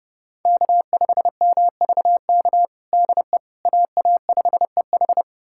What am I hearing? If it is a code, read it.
Morse code, heard as a keyed tone, about 30 wpm, 700 Hz. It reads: K5MVK DE AA5EH
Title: USA Callsign DE Callsign - 5 Characters - 30wpm